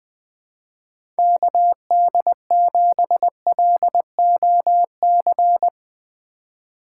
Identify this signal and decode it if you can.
Morse code, heard as a keyed tone, about 20 wpm, 700 Hz. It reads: KD7LOC
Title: USA Call Signs - 6 Characters - 3x Character Spacing - 20wpm